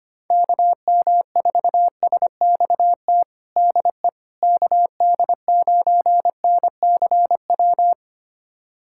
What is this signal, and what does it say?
Morse code, heard as a keyed tone, about 25 wpm, 700 Hz. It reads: KM4SXT DE KD9NCW
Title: USA Callsign DE Callsign - 6 Characters - 25wpm